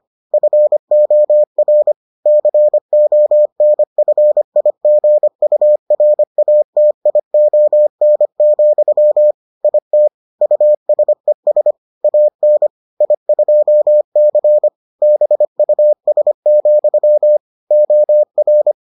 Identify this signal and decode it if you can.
Morse code, heard as a keyed tone, about 25 wpm, 600 Hz. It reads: FOR CONFIGURATION, IT USES AN I2C BUS, OR